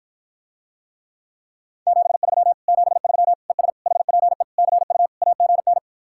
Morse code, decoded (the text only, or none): 73 73 ES HPE CU AGN